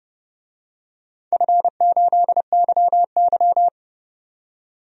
Morse code, heard as a keyed tone, about 30 wpm, 700 Hz. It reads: F8YY
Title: DXCC-100 Call Signs - 4 Characters - 30wpm